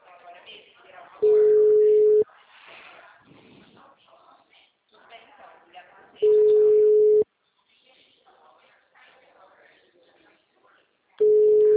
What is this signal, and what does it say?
Ringback tone (Europe, 425 Hz), with speech.